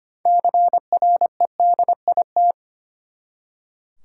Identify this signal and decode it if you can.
Morse code, heard as a keyed tone, about 25 wpm, 700 Hz. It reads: CREDIT
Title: Six Letter Words - 25wpm